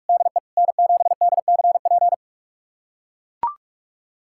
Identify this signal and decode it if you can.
Morse code, heard as a keyed tone, about 42 wpm, 700 Hz. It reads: DE N7DCP
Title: USA Callsign DE Callsign - 5 Characters - 45wpm